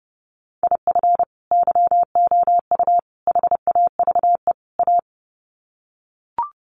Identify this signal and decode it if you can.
Morse code, heard as a keyed tone, about 30 wpm, 700 Hz. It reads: IF YOU HAVE A
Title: Sets of 4 Words 30wpm